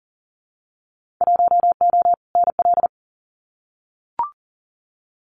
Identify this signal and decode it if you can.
Morse code, heard as a keyed tone, about 40 wpm, 700 Hz. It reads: 1O NL
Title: Winter Field Day Exchanges Encore 40wpm